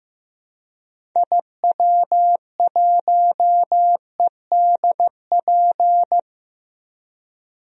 Morse code, heard as a keyed tone, about 15 wpm, 700 Hz. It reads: IW1EDP
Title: DXCC-100 Call Signs - 6 Characters - Repeated 3x - 15wpm